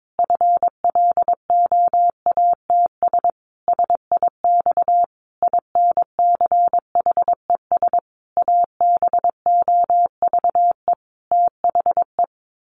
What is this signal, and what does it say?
Morse code, heard as a keyed tone, about 22 wpm, 700 Hz. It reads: FLOATS SIX INCHES ABOVE THE
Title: A Sound Of Thunder 22wpm